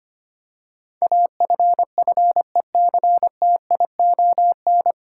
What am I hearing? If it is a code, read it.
Morse code, heard as a keyed tone, about 25 wpm, 700 Hz. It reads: AFFECTION
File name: Nine Letter Words - 25wpm